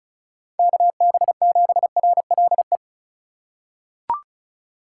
Morse code, heard as a keyed tone, about 34 wpm, 700 Hz. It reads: KB7RLE
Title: USA Call Signs - 6 Characters - 2x Character Spacing - 35wpm